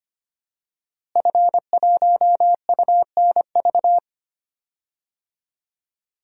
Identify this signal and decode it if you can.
Morse code, heard as a keyed tone, about 25 wpm, 700 Hz. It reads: F1UNV
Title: DXCC-100 Call Signs - 5 Characters - Repeated 3x - 25wpm